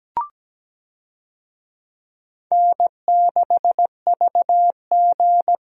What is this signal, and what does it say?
Morse code, heard as a keyed tone, about 17 wpm, 700 Hz. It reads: N6VG
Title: USA Call Signs - 4 Characters - Repeated 3x - 17wpm